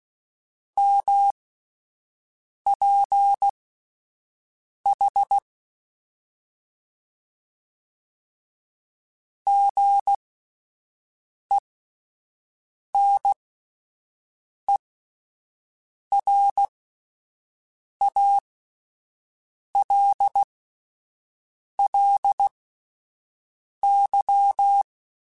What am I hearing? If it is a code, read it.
Morse code, heard as a keyed tone, about 16 wpm, spaced as for 5 wpm, 800 Hz. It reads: MPH GENERALLY
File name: Spaced Characters / Spaced Characters-mp3s / Weather 4 14 sp @10